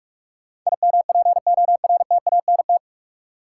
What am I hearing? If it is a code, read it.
Morse code, heard as a keyed tone, about 46 wpm, 700 Hz. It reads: IMPORTANT
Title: Sentences from Top 600 Words - New Word Drill - 45wpm